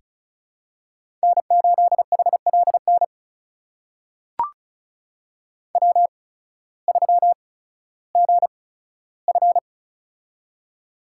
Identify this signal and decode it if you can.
Morse code, heard as a keyed tone, about 36 wpm, spaced as for 16 wpm, 700 Hz. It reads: N8HLN W3GF
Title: USA Call Signs - 8x Character Spacing - 35wpm